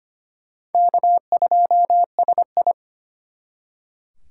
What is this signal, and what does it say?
Morse code, heard as a keyed tone, about 31 wpm, 700 Hz. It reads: K2SI
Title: Speed Racing - USA 1x2 Call Signs 25wpm